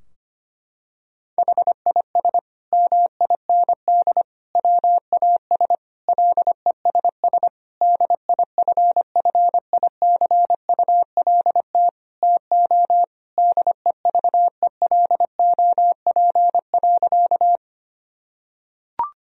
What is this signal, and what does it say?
Morse code, heard as a keyed tone, about 25 wpm, 700 Hz. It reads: HIS MIND WAS LESS DIFFICULT TO DEVELOP.
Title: Sentences from Top 1000 Words - Review Part 2 - 25wpm